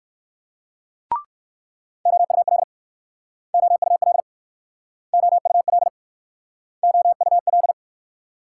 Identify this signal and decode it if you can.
Morse code, heard as a keyed tone, about 57 wpm, 700 Hz. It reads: OUL OUL OUL OUL